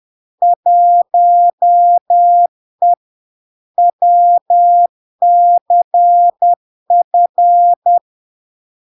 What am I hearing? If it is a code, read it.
Morse code, heard as a keyed tone, about 10 wpm, 700 Hz. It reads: 1E WCF
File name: LARK-FD-2022-Log-in-CW-10wpm-700-Hz